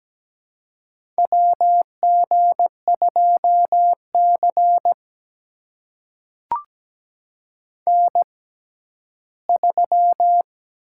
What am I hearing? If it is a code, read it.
Morse code, heard as a keyed tone, about 17 wpm, spaced as for 11 wpm, 700 Hz. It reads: WG2C N3
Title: USA Call Signs - 4 Characters - 6x Character Spacing - 17wpm